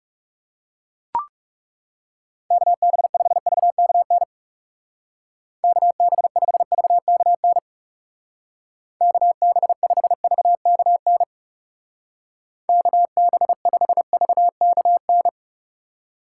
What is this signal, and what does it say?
Morse code, heard as a keyed tone, about 35 wpm, 700 Hz. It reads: KB5VKN KB5VKN KB5VKN KB5VKN
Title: Speed Racing - USA Call Signs 30wpm